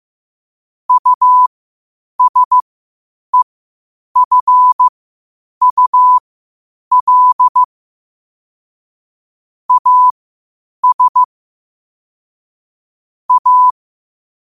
Morse code, heard as a keyed tone, about 15 wpm, spaced as for 8 wpm, 1000 Hz. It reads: USEFUL AS A
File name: qotdfiles / 8wpm